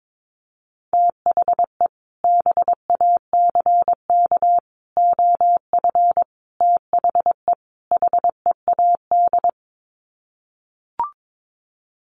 Morse code, heard as a keyed tone, about 22 wpm, 700 Hz. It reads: THE BACK OF THE HEAD
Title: Sets of 5 Words Encore2 22wpm